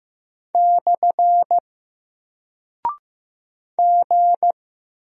Morse code, heard as a keyed tone, about 15 wpm, 700 Hz. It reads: / G
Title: Single Character (Letter:Number:Punctuation:Prosign) Alt 15wpm